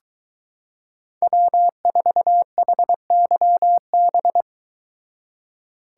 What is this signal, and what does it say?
Morse code, heard as a keyed tone, about 22 wpm, 700 Hz. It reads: W4HYB